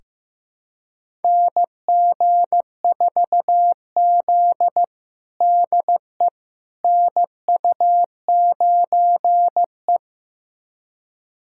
Morse code, heard as a keyed tone, about 15 wpm, 700 Hz. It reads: NG4Z DE NU9E
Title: USA Callsign DE Callsign - 4 Characters - 15wpm